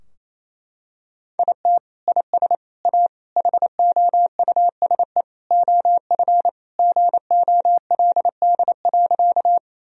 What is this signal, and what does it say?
Morse code, heard as a keyed tone, about 27 wpm, 700 Hz. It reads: IT IS A HOUSE OF GOLD.
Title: Sentences from Top 1000 Words - Review Part 3 - 28wpm